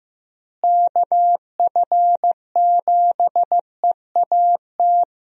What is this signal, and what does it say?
Morse code, heard as a keyed tone, about 15 wpm, 700 Hz. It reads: KF7EAT
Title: USA Call Signs - 6 Characters - Repeated 3x - 15wpm